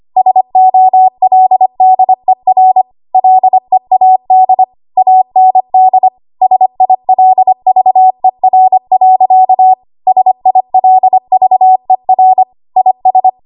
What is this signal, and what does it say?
Morse code, heard as a keyed tone, about 25 wpm, 750 Hz. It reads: SOLDER LEAD AND SILVER. SILVER IS
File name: Archive / 25 WPM / 25WPM